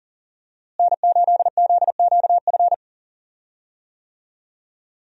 Morse code, heard as a keyed tone, about 40 wpm, 700 Hz. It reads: N8ZQF
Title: USA Call Signs - 5 Characters - Repeated 3x - 40wpm